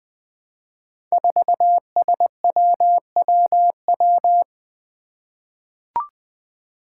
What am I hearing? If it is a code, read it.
Morse code, heard as a keyed tone, about 20 wpm, 700 Hz. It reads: 4SWWW
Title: Five Challenging Letter-Number Combinations 20wpm